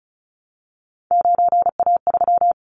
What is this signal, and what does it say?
Morse code, heard as a keyed tone, about 36 wpm, 700 Hz. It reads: 9A3
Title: Callsign Prefixes for Top 25 DXCC - 35wpm